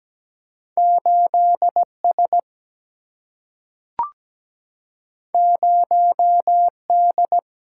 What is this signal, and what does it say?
Morse code, heard as a keyed tone, about 17 wpm, 700 Hz. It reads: 8 S 0 D
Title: Single-Letter Single-Number 17wpm